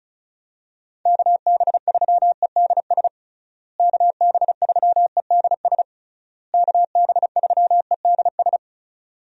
Morse code, heard as a keyed tone, about 36 wpm, 700 Hz. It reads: KB3EDS KB3EDS KB3EDS
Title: USA Call Signs - 6 Characters - Repeated 3x - 35wpm